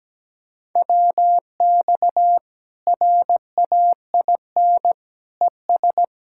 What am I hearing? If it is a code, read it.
Morse code, heard as a keyed tone, about 17 wpm, 700 Hz. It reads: WX RAIN ES
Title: Practice QSO - Exchange 2 - Info - 17wpm